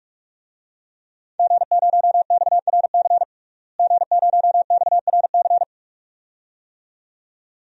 Morse code, heard as a keyed tone, about 42 wpm, 700 Hz. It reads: G0XRC G0XRC
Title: DXCC-100 Call Signs - 5 Characters - Repeated 2x - 45wpm